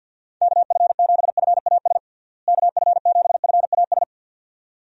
Morse code, heard as a keyed tone, about 50 wpm, 700 Hz. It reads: KF7FAS KF7FAS
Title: USA Call Signs - 6 Characters - Repeated 2x - 50wpm